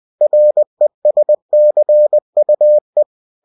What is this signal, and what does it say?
Morse code, heard as a keyed tone, about 20 wpm, 600 Hz. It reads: RESCUE
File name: cw-books / winnie-the-pooh / winnie-20wpm-page-0098